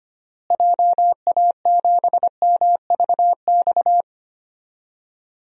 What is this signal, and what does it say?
Morse code, heard as a keyed tone, about 25 wpm, 700 Hz. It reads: JA7MVX
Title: DXCC-100 Call Signs - 6 Characters - Repeated 3x - 25wpm